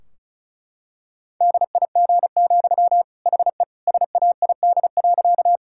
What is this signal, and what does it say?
Morse code, heard as a keyed tone, about 34 wpm, 700 Hz. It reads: DIG, HE SAID.
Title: Sentences from Top 1600 Words 35wpm